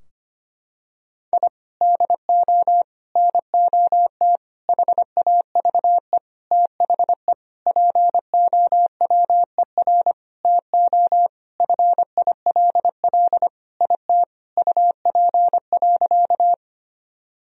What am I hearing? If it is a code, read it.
Morse code, heard as a keyed tone, about 25 wpm, 700 Hz. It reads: I DO NOT HAVE THE POWER TO FILL IT UP.
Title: Sentences from Top 1000 Words - Review Part 3 - 25wpm